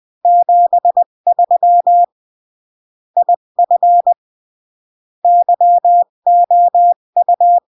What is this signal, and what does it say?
Morse code, heard as a keyed tone, about 20 wpm, 700 Hz. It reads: 73 IF YOU
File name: CW / jodorowsky / 82 commandments 20-15